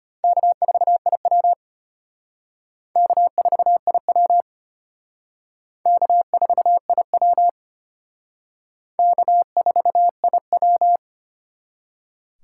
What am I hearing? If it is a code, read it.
Morse code, heard as a keyed tone, about 31 wpm, 700 Hz. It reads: K4IW K4IW K4IW K4IW
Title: Speed Racing - USA 1x2 Call Signs 25wpm